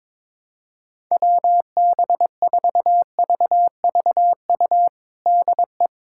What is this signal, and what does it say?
Morse code, heard as a keyed tone, about 21 wpm, 700 Hz. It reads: WB4VVU DE
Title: USA Callsign DE Callsign - 6 Characters - 22wpm